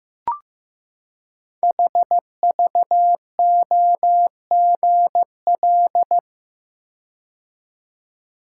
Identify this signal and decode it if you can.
Morse code, heard as a keyed tone, about 15 wpm, 700 Hz. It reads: HVOGL
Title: Code Groups - 5 Letter:Numbers - 15wpm